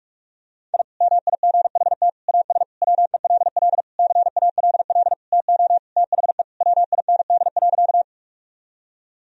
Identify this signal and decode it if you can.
Morse code, heard as a keyed tone, about 46 wpm, 700 Hz. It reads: I MIGHT AS WELL CALL TO THE WIND.
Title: Sentences from Top 1000 Words - Review Part 2 - 45wpm